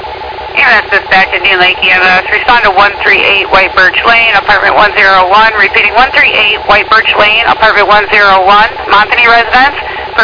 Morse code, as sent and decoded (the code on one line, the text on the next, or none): none